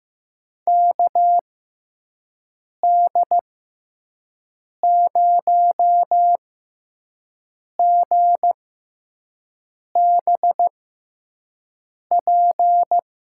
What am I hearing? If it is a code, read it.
Morse code, heard as a keyed tone, about 15 wpm, spaced as for 5 wpm, 700 Hz. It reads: KD0GBP
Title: USA Call Signs - 6 Characters - 6x Character Spacing - 15wpm